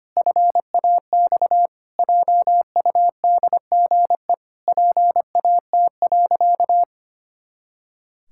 Morse code, heard as a keyed tone, about 25 wpm, 700 Hz. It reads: FAX JUDGE PAT.
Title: Pangram Challenge v2 - 25wpm